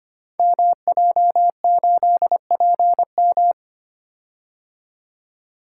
Morse code, heard as a keyed tone, about 25 wpm, 700 Hz. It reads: MJ8PM
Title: Code Groups - 5 Letter:Numbers - 25wpm